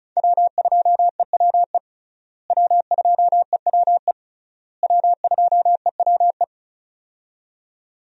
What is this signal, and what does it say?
Morse code, heard as a keyed tone, about 36 wpm, 700 Hz. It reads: W2EWE W2EWE W2EWE
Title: USA Call Signs - 5 Characters - Repeated 3x - 35wpm